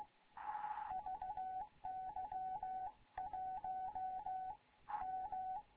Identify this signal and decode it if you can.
Morse code, heard as a keyed tone, about 16 wpm, 750 Hz. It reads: VY 1 M
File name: Morze-HA2MN / hangp / 2 11 GYAKORLAT